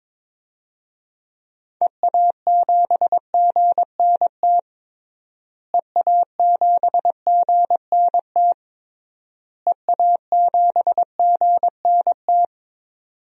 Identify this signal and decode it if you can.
Morse code, heard as a keyed tone, about 22 wpm, 700 Hz. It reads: EA7GNT EA7GNT EA7GNT
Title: DXCC-100 Call Signs - 6 Characters - Repeated 3x - 22wpm